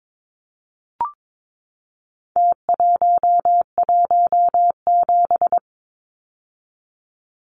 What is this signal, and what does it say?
Morse code, heard as a keyed tone, about 22 wpm, 700 Hz. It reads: T117
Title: Numbers - Contest Style - 4-Digit 22wpm